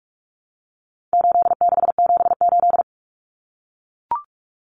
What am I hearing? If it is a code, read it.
Morse code, heard as a keyed tone, about 44 wpm, 700 Hz. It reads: 8678
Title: Numbers - 4-Digit 45wpm